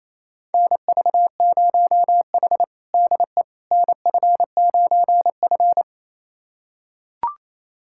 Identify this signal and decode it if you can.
Morse code, heard as a keyed tone, about 28 wpm, 700 Hz. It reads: NV0H DE NF9F
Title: USA Callsign DE Callsign - 4 Characters - 28wpm